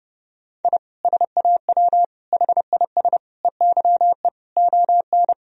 Morse code, heard as a keyed tone, about 30 wpm, 700 Hz. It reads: I SAW HIS EYE ON
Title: Sentences from Top 1000 Words - Review Part 1 - 30wpm